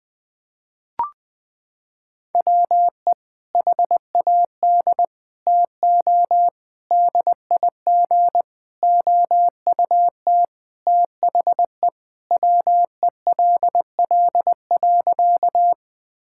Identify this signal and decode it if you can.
Morse code, heard as a keyed tone, about 21 wpm, 700 Hz. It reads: WE HAD TO DIG OUT THE WELL.
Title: Sentences from Top 1600 Words 20wpm